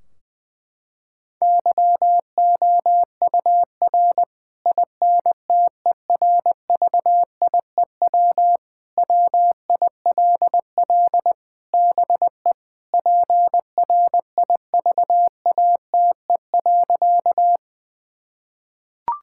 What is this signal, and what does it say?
Morse code, heard as a keyed tone, about 20 wpm, 700 Hz. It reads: YOUR INTERVIEW WILL BE PRIVATE.
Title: Sentences from Top 1600 Words 20wpm